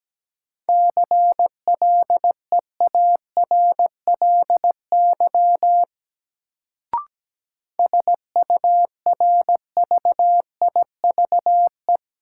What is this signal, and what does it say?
Morse code, heard as a keyed tone, about 17 wpm, 700 Hz. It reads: CLEARLY SURVIVE
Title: Sentences from Top 1300 Words - New Word Drill - 17wpm